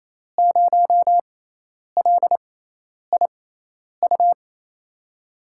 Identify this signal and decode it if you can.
Morse code, heard as a keyed tone, about 28 wpm, spaced as for 10 wpm, 700 Hz. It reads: 0LIU
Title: USA Call Signs - 6 Characters - 6x Character Spacing - 28wpm